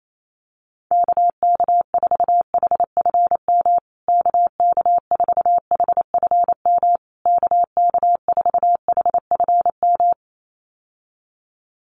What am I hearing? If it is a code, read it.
Morse code, heard as a keyed tone, about 28 wpm, 700 Hz. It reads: KK4HFM KK4HFM KK4HFM
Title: USA Call Signs Repeated 3x 28wpm